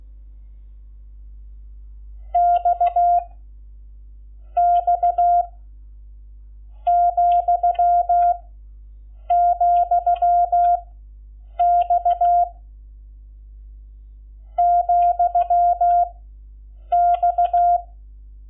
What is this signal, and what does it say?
Morse code, heard as a keyed tone, about 16 wpm, 700 Hz. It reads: X X , , X , X
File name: Lecciones / CLASE Nº11